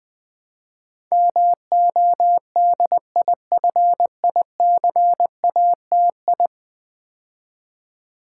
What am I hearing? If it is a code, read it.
Morse code, heard as a keyed tone, about 20 wpm, 700 Hz. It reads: MODIFICATI
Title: Build A Word - 5 Letters At A Time - 12-Letter Words - 20wpm